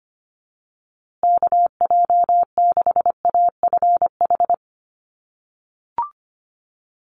Morse code, heard as a keyed tone, about 25 wpm, 700 Hz. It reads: KJ6AFH
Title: USA Call Signs Repeated 3x 25wpm